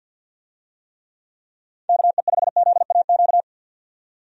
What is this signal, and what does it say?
Morse code, heard as a keyed tone, about 52 wpm, 700 Hz. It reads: KE5ZAQ
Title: USA Call Signs - 6 Characters - 50wpm